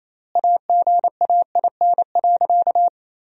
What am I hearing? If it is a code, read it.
Morse code, heard as a keyed tone, about 28 wpm, 700 Hz. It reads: AGAIN.
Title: 4x Word-Spacing - Sentences from Top 300 Words - 28wpm